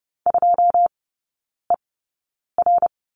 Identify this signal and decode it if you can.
Morse code, heard as a keyed tone, about 30 wpm, spaced as for 9 wpm, 700 Hz. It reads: 2ER
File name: Lesson 18e - Letter m - Three Characters 30wpm